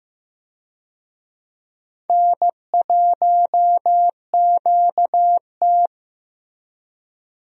Morse code, heard as a keyed tone, about 15 wpm, 700 Hz. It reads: N1QT
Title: USA Call Signs - 4 Characters - Repeated 2x - 15wpm